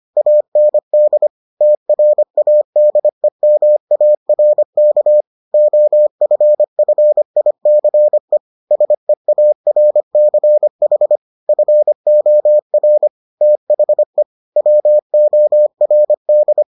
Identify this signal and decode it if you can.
Morse code, heard as a keyed tone, about 25 wpm, 600 Hz. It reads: AND TRADEMARK OFFICE SEARCH FOR THE WORD